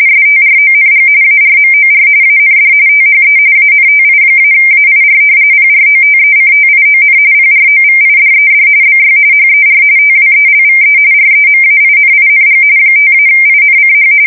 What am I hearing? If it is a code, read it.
RTTY, about 45.45 baud. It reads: STRONG ARMOR AND THE ANTI-AIR PROTECTION PROVIDED BY HER TWELVE REMAINING 5-INCH
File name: fsk170 / battleshipnj